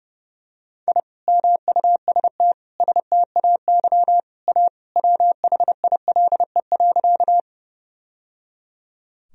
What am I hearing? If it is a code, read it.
Morse code, heard as a keyed tone, about 29 wpm, 700 Hz. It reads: I MUST STAY A WHILE.
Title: Sentences from Top 1000 Words - Review Part 1 - 30wpm